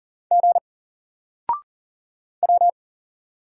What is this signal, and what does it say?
Morse code, heard as a keyed tone, about 40 wpm, 700 Hz. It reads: G W
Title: Single Letter-Number Alt 40wpm